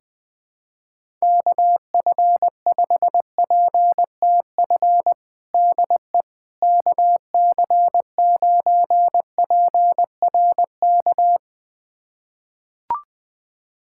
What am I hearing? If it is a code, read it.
Morse code, heard as a keyed tone, about 20 wpm, 700 Hz. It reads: KF5PTF DE KC9PRK
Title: USA Callsign DE Callsign - 6 Characters - 20wpm